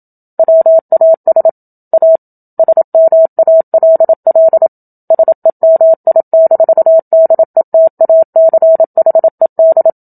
Morse code, heard as a keyed tone, about 27 wpm, 650 Hz. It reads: WAS A SMALL SEMI-DETACHED
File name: Book Two / 01-Chapter 1 - 27 WPM